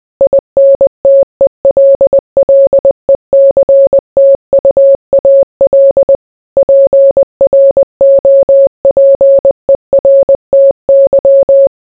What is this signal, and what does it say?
Morse code, heard as a keyed tone, about 20 wpm, 550 Hz. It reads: INTELLECTUAL PROPERTY